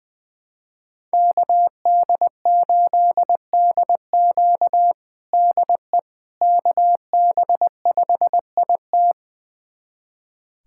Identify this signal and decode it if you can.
Morse code, heard as a keyed tone, about 20 wpm, 700 Hz. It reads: KD8DQ DE KB5IT
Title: USA Callsign DE Callsign - 5 Characters - 20wpm